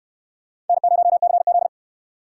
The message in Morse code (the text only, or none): N0CZ